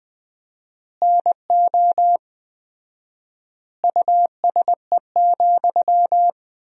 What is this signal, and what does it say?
Morse code, heard as a keyed tone, about 20 wpm, 700 Hz. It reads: NO USE,
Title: 4x Word-Spacing - Sentences from Top 500 Words - 20wpm